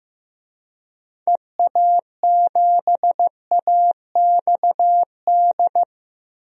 Morse code, heard as a keyed tone, about 15 wpm, 700 Hz. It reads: EA7AXD
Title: DXCC-100 Call Signs - 6 Characters - Repeated 3x - 15wpm